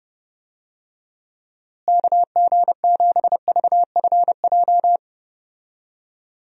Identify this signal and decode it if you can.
Morse code, heard as a keyed tone, about 32 wpm, 700 Hz. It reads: KG7VFJ
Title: USA Call Signs - 6 Characters - Repeated 3x - 30wpm